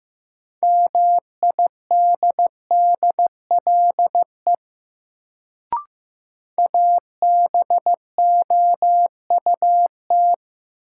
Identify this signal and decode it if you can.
Morse code, heard as a keyed tone, about 15 wpm, 700 Hz. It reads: MIDDLE ABOUT
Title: Sets of 2 Words - New Word Drill - 15wpm